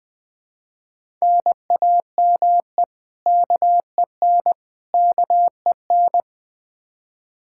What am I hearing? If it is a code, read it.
Morse code, heard as a keyed tone, about 20 wpm, 700 Hz. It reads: NAME KEN KEN
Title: Practice QSO - Exchange 1 - Beginning - 20wpm